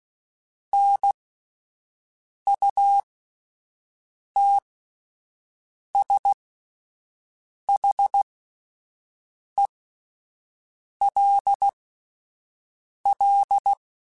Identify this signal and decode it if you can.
Morse code, heard as a keyed tone, about 16 wpm, spaced as for 5 wpm, 800 Hz. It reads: NUTSHELL